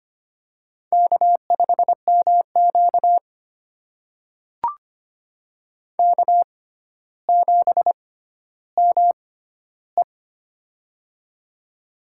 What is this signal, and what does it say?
Morse code, heard as a keyed tone, about 25 wpm, spaced as for 13 wpm, 700 Hz. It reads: K5MQ K7ME
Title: USA Call Signs - 4 Characters - 6x Character Spacing - 25wpm